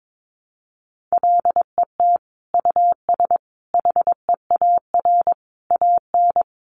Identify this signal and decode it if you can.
Morse code, heard as a keyed tone, about 22 wpm, 700 Hz. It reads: LET US HEAR AN
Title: Sentences from Top 900 Words 22wpm